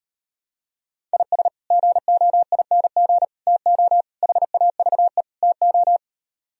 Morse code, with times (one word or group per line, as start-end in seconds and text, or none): 1.1-1.5 IS
1.7-3.2 GOING
3.5-4.0 TO
4.2-5.2 HAVE
5.4-6.0 TO